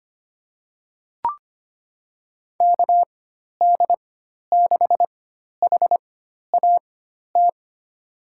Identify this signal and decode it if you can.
Morse code, heard as a keyed tone, about 25 wpm, spaced as for 12 wpm, 700 Hz. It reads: KD6HAT
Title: USA Call Signs - 6 Characters - 4x Character Spacing - 25wpm